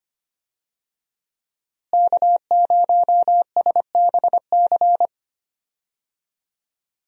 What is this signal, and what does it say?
Morse code, heard as a keyed tone, about 25 wpm, 700 Hz. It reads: K0SBC
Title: USA Call Signs - 5 Characters - Repeated 3x - 25wpm